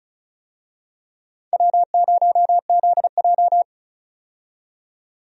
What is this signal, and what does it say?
Morse code, heard as a keyed tone, about 34 wpm, 700 Hz. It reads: W0ZJ
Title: USA Call Signs - 4 Characters - 35wpm